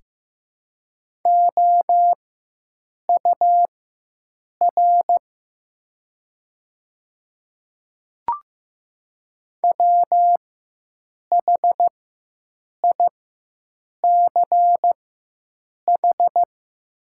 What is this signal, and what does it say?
Morse code, heard as a keyed tone, about 16 wpm, spaced as for 7 wpm, 700 Hz. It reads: OUR WHICH
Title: 4x Character Spacing - Top 300 Words - 15wpm